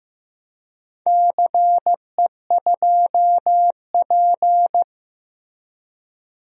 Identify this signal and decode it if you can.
Morse code, heard as a keyed tone, about 15 wpm, 700 Hz. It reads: CE2P
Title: DXCC-100 Call Signs - 4 Characters - Repeated 3x - 15wpm